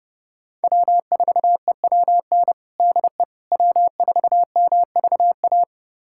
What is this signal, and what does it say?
Morse code, heard as a keyed tone, about 30 wpm, 700 Hz. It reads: W4EWN DE W4MVA
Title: USA Callsign DE Callsign - 5 Characters - 30wpm